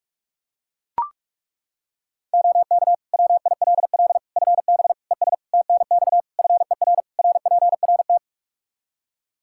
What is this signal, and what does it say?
Morse code, heard as a keyed tone, about 42 wpm, 700 Hz. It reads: OK WILL FB ES TNX FER RPRT
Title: Practice QSO - Exchange 2 - Info - 45wpm